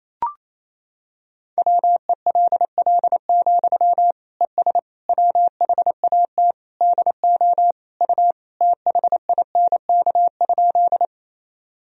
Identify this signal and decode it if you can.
Morse code, heard as a keyed tone, about 28 wpm, 700 Hz. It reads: WELL, ES WHAT DO U THINK?
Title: Sentences from Top 200 Words and Abbreviations 28wpm